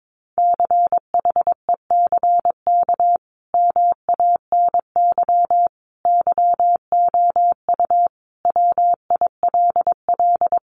Morse code, heard as a keyed tone, about 22 wpm, 700 Hz. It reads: CHECK MANY YOU WILL
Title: Sentences from Top 1500 Words 22wpm